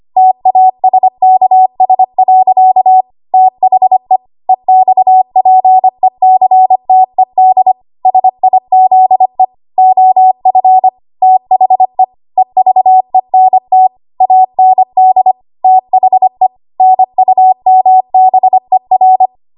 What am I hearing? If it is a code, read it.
Morse code, heard as a keyed tone, about 25 wpm, 750 Hz. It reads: TASKS. THE EXPECTED SIZE OF THE EVENT AND THE NUMBER
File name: Morse / 25WPM